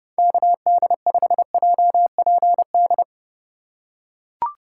Morse code, heard as a keyed tone, about 30 wpm, 700 Hz. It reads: KD5JPD